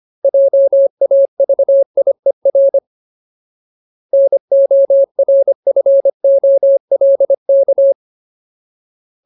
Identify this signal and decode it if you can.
Morse code, heard as a keyed tone, about 25 wpm, 550 Hz. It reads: JAVIER NORFOLK
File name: Ultimate Word List / 25WPM Ultimate